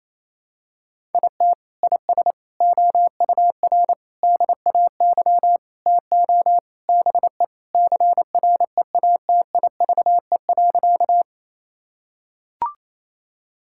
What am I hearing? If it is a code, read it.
Morse code, heard as a keyed tone, about 28 wpm, 700 Hz. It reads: IT IS OUR DAY TO BE CREATIVE.
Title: Sentences from Top 1000 Words - Review Part 5 - 28wpm